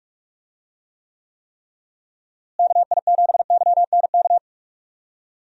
Morse code, heard as a keyed tone, about 44 wpm, 700 Hz. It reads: KI7YNK
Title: USA Call Signs - 6 Characters - Repeated 3x - 45wpm